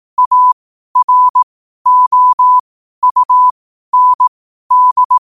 Morse code, heard as a keyed tone, about 18 wpm, spaced as for 12 wpm, 1000 Hz. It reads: AROUND
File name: qotdfiles / 12wpm